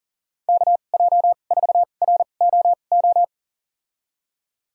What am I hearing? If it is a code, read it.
Morse code, heard as a keyed tone, about 39 wpm, spaced as for 29 wpm, 700 Hz. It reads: KJ4ROO
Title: USA Call Signs - 6 Characters - 2x Character Spacing - 40wpm